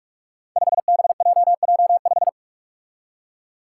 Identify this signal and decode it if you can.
Morse code, heard as a keyed tone, about 46 wpm, 700 Hz. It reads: 5BJJ5
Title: Five Challenging Letter-Number Combinations 45wpm